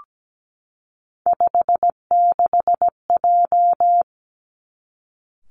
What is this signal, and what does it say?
Morse code, heard as a keyed tone, about 17 wpm, 700 Hz. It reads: 56J
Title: Three Challenging Letter-Number Combinations 17wpm